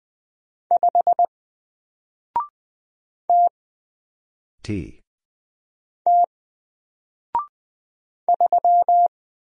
Morse code, heard as a keyed tone, about 20 wpm, 700 Hz. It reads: 5 T T 3
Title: Single Letter-Number Alt 20wpm